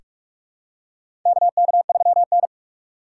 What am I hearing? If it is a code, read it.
Morse code, heard as a keyed tone, about 44 wpm, 700 Hz. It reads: KK3N
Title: USA Call Signs - 4 Characters - 6x Character Spacing - 45wpm